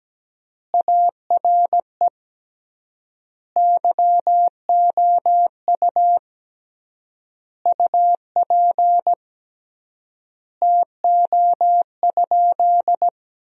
Morse code, heard as a keyed tone, about 17 wpm, 700 Hz. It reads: ARE YOU UP TO?
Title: 3x Word-Spacing - Sentences from Top 100 Words - 17wpm